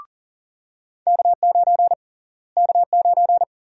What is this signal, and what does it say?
Morse code, heard as a keyed tone, about 40 wpm, 700 Hz. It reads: K9 K9
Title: USA Call Sign Prefixes - Repeated 2x - 40wpm